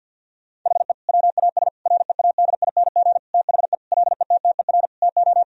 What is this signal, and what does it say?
Morse code, heard as a keyed tone, about 50 wpm, 700 Hz. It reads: HE WAS READING THE LETTER TO
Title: Sentences from Top 1000 Words - Review Part 3 - 50wpm